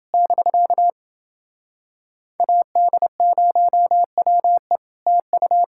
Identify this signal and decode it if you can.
Morse code, heard as a keyed tone, about 30 wpm, 700 Hz. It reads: <BK> AD0WE TU